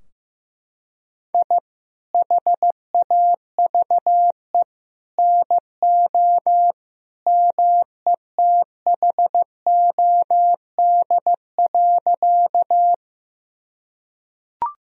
Morse code, heard as a keyed tone, about 16 wpm, 700 Hz. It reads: I HAVE NO METHOD.
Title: Sentences from Top 1000 Words - Review Part 3 - 15wpm